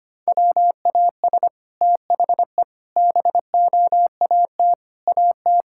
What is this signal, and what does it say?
Morse code, heard as a keyed tone, about 25 wpm, 700 Hz. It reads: WAS THE BOAT AT
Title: Sentences from Top 1000 Words - Review Part 5 - 25wpm